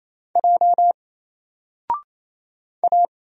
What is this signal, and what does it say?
Morse code, heard as a keyed tone, about 27 wpm, 700 Hz. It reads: J A
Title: Single Character (Letter:Number:Punctuation:Prosign) 28wpm